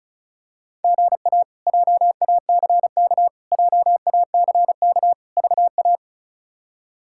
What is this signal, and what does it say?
Morse code, heard as a keyed tone, about 36 wpm, 700 Hz. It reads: GA JACK JACK VA
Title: K1USN SST QSO Practice - 35wpm